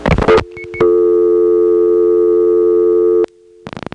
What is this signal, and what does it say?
Dial tone (North America), 350 + 440 Hz.